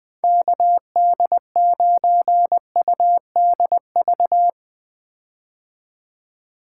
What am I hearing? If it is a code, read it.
Morse code, heard as a keyed tone, about 21 wpm, 700 Hz. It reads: KD9UDV
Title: 2023 POTA Activator Call Signs - 20wpm